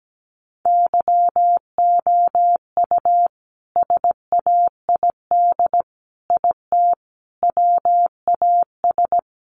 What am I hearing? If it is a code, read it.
Morse code, heard as a keyed tone, about 17 wpm, 700 Hz. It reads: YOU SAID IT WAS
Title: Sentences from Top 1100 Words 17wpm